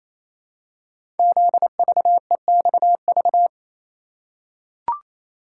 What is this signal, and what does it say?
Morse code, heard as a keyed tone, about 27 wpm, 700 Hz. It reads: ZVEXV
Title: Code Groups - 5 Letters - 28wpm